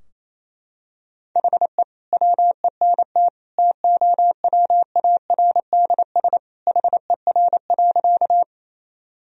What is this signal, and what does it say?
Morse code, heard as a keyed tone, about 28 wpm, 690 Hz. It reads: HE WENT TOWARDS HER.
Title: Sentences from Top 1000 Words - Review Part 2 - 28wpm